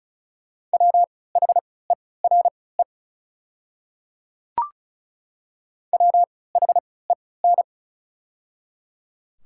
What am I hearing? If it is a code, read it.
Morse code, heard as a keyed tone, about 36 wpm, spaced as for 20 wpm, 700 Hz. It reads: WHERE WHEN
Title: 3x Character Spacing - Top 200 Words - 35wpm